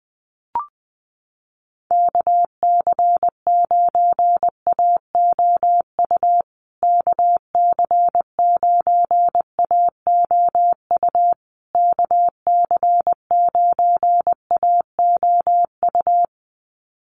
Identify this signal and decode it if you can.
Morse code, heard as a keyed tone, about 20 wpm, 700 Hz. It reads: KC9AOU KC9AOU KC9AOU
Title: USA Call Signs Repeated 3x 20wpm